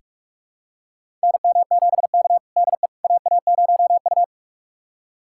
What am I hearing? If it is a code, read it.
Morse code, heard as a keyed tone, about 46 wpm, 700 Hz. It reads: NM7K DE AA0U
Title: USA Callsign DE Callsign - 4 Characters - 45wpm